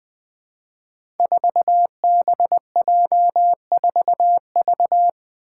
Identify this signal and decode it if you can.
Morse code, heard as a keyed tone, about 20 wpm, 700 Hz. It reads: 4BJ4V
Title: Five Challenging Letter-Number Combinations 20wpm